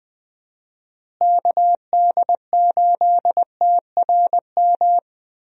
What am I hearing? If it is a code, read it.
Morse code, heard as a keyed tone, about 20 wpm, 700 Hz. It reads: KD8TRM
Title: USA Call Signs - 6 Characters - Repeated 2x - 20wpm